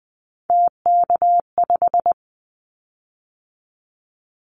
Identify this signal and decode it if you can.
Morse code, heard as a keyed tone, about 20 wpm, 700 Hz. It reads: TK5
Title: Callsign Prefixes for Top 100 DXCC - 20wpm